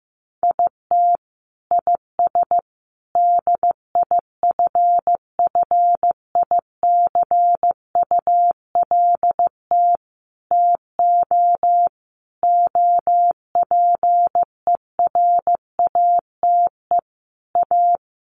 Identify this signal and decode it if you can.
Morse code, heard as a keyed tone, about 14 wpm, 700 Hz. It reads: IT IS DIFFICULT TO OPERATE A
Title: Sentences from Top 1500 Words 15wpm